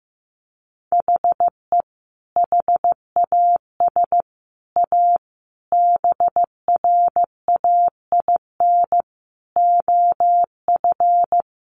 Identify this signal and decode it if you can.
Morse code, heard as a keyed tone, about 15 wpm, 700 Hz. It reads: HE HAS A BRAIN OF
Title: Sentences from Top 1100 Words 15wpm